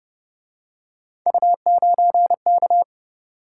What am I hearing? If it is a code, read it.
Morse code, heard as a keyed tone, about 30 wpm, 700 Hz. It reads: U9K
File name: Speed-Racing - Three Letter-Number Combinations - 25wpm